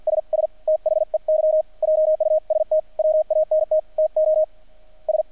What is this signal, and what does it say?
Morse code, heard as a keyed tone, about 50 wpm, 620 Hz, with radio noise. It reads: SS THEY JUST WANT TO S
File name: archived / CRSnet